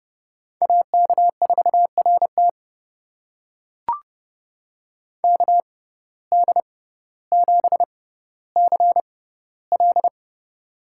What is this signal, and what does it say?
Morse code, heard as a keyed tone, about 30 wpm, spaced as for 15 wpm, 700 Hz. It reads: AK4RT KD7CL
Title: USA Call Signs - 5 Characters - 6x Character Spacing - 30wpm